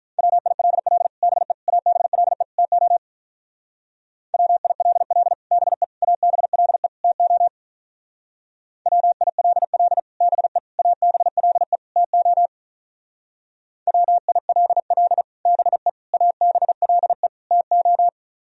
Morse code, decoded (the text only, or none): WILL BE ABLE TO WILL BE ABLE TO WILL BE ABLE TO WILL BE ABLE TO